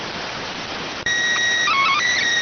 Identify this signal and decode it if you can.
SSTV (slow-scan television), PD 290.